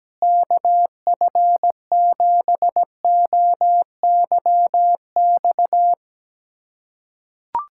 Morse code, heard as a keyed tone, about 17 wpm, 700 Hz. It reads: KF7OYX
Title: USA Callsign DE Callsign - 6 Characters - 17wpm